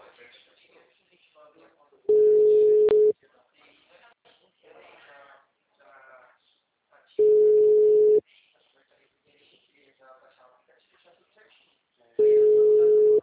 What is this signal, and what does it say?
Ringback tone (Europe, 425 Hz), with speech.
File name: out- -2006- - - .22